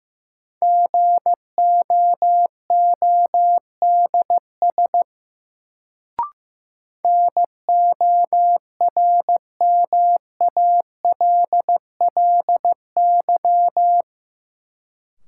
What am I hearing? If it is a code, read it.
Morse code, heard as a keyed tone, about 15 wpm, 700 Hz. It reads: GOODS NORMALLY
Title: Sentences from Top 1000 Words - Review Part 6 - New Word Drill - 15wpm